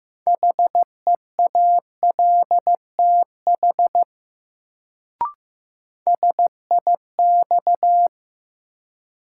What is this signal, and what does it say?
Morse code, heard as a keyed tone, about 15 wpm, 700 Hz. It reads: HEALTH SIX
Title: Sentences from Top 600 Words - New Word Drill - 15wpm